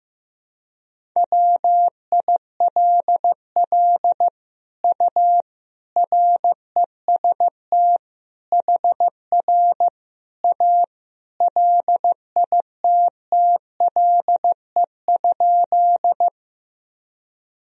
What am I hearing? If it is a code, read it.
Morse code, heard as a keyed tone, about 15 wpm, 700 Hz. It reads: WILL U REST HR A LITTLE?
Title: Sentences from Top 500 Words and Abbreviations 15wpm